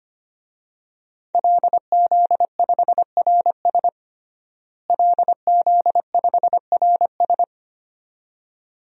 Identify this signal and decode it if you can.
Morse code, heard as a keyed tone, about 25 wpm, 700 Hz. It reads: LZ5RS LZ5RS
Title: DXCC-100 Call Signs - 5 Characters - Repeated 2x - 25wpm